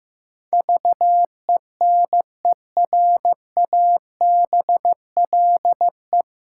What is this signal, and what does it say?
Morse code, heard as a keyed tone, about 15 wpm, 700 Hz. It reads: VENERABLE
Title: Pangram Challenge v2 - 15wpm